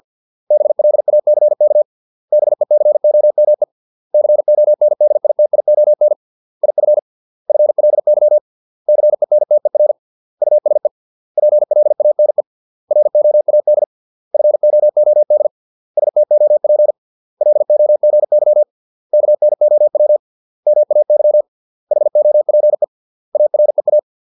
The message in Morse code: BLACK BECOME CONDITION IF FLY CENTER USE PLANE ROAD FOOD STOP LOGY KNOW MAY HOPE AREA

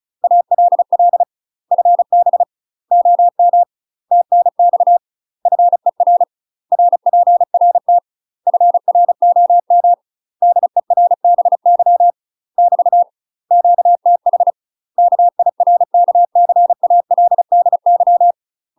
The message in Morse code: ALL FB OM TNX FER RPRT FROM DERBY <BT> QTH KIRKCALDY

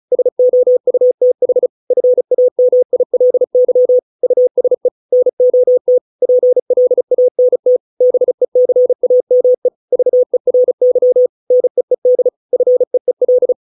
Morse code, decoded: SOUTH FAMILY USE NOT PLANT BECAME VERY NEED FEEL

